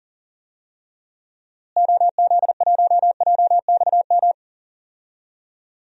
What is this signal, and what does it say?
Morse code, heard as a keyed tone, about 40 wpm, 700 Hz. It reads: OZ1JXM
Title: DXCC-100 Call Signs - 6 Characters - Repeated 2x - 40wpm